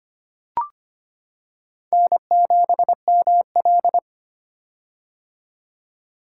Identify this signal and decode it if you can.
Morse code, heard as a keyed tone, about 25 wpm, 700 Hz. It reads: N7ML
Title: K1USN SST QSO Practice - 25wpm